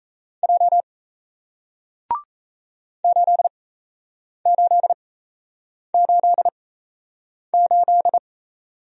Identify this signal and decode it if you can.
Morse code, heard as a keyed tone, about 38 wpm, spaced as for 8 wpm, 700 Hz. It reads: J 8888